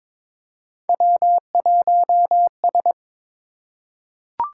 Morse code, heard as a keyed tone, about 22 wpm, 700 Hz. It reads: W1S
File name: practice / 22wpm / Three Letter-Number Combinations - 22wpm